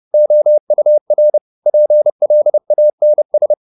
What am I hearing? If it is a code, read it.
Morse code, heard as a keyed tone, about 30 wpm, 600 Hz. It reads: OUR PLANS